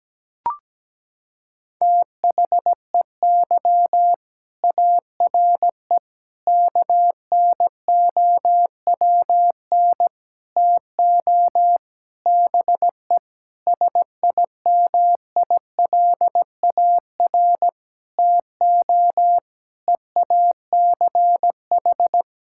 Morse code, heard as a keyed tone, about 17 wpm, 700 Hz. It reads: THEY ARE KNOWN TO BE SIMILAR TO EACH
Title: Sentences from Top 1000 Words - Review Part 5 - 17wpm